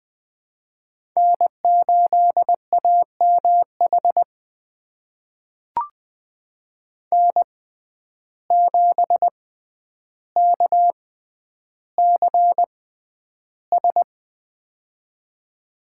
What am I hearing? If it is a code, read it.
Morse code, heard as a keyed tone, about 20 wpm, spaced as for 10 wpm, 700 Hz. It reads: N8AMH N7KCS
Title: USA Call Signs - 5 Characters - 6x Character Spacing - 20wpm